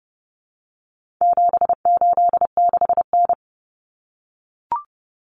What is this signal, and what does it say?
Morse code, heard as a keyed tone, about 30 wpm, 700 Hz. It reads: 786N
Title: Common Cut Numbers - 4-Digit 30wpm